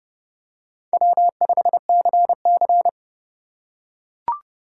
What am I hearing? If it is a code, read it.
Morse code, heard as a keyed tone, about 31 wpm, 700 Hz. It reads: W5CC
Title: USA Call Signs - 4 Characters - 4x Character Spacing - 30wpm